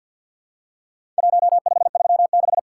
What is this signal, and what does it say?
Morse code, heard as a keyed tone, about 50 wpm, 700 Hz. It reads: 1536
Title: Worked All Europe DX Contest - QTC Exchange - 50wpm